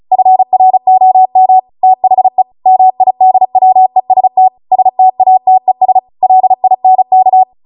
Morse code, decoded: FROM THE MIDWEST STATES LINK